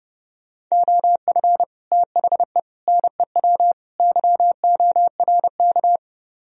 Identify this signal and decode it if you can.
Morse code, heard as a keyed tone, about 38 wpm, 700 Hz. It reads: OF THE NEW YORK